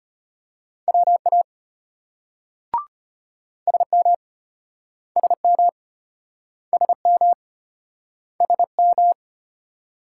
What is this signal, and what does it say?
Morse code, heard as a keyed tone, about 34 wpm, 700 Hz. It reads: WA SM SM SM SM